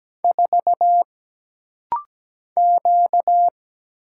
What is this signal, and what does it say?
Morse code, heard as a keyed tone, about 17 wpm, 700 Hz. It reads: 4 Q
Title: Single Letter-Number - Rapid-Fire 17wpm